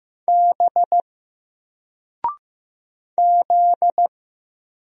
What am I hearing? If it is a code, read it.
Morse code, heard as a keyed tone, about 15 wpm, 700 Hz. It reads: B Z
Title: Single Letter-Number 15wpm